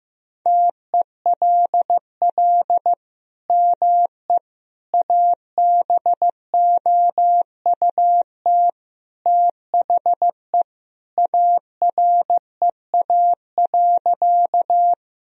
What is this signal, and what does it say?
Morse code, heard as a keyed tone, about 15 wpm, 700 Hz. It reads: TELL ME ABOUT THE AREA.
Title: Sentences from Top 1000 Words - Review Part 2 - 15wpm